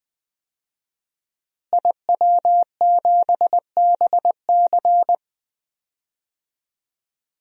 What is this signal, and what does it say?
Morse code, heard as a keyed tone, about 20 wpm, 700 Hz. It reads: IW7BC
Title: DXCC-100 Call Signs - 5 Characters - Repeated 3x - 20wpm